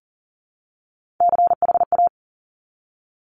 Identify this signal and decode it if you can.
Morse code, heard as a keyed tone, about 39 wpm, 700 Hz. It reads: CHA
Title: Top Three Letter Combinations in Words 40wpm